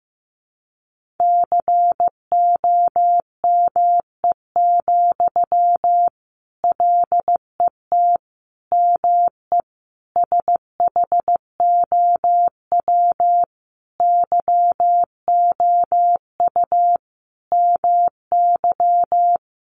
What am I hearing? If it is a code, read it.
Morse code, heard as a keyed tone, about 15 wpm, 700 Hz. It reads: COME, LET ME SHOW YOU MY
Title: Sentences from Top 500 Words 15wpm